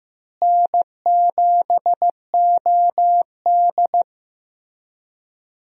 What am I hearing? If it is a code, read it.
Morse code, heard as a keyed tone, about 15 wpm, 700 Hz. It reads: N7OD
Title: USA Call Signs - 4 Characters - Repeated 3x - 15wpm